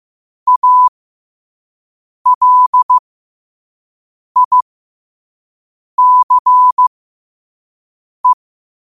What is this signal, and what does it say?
Morse code, heard as a keyed tone, about 15 wpm, spaced as for 5 wpm, 1000 Hz. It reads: ALICE
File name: qotdfiles / 5wpm